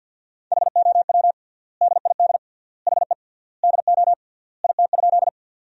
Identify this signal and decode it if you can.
Morse code, heard as a keyed tone, about 50 wpm, 700 Hz. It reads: HOW DID HE DO IT?
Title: 3x Word-Spacing - Sentences from Top 100 Words - 50wpm